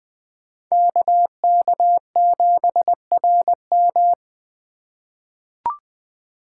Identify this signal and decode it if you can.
Morse code, heard as a keyed tone, about 20 wpm, 700 Hz. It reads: KK7RM
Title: USA Call Signs - 5 Characters - 3x Character Spacing - 20wpm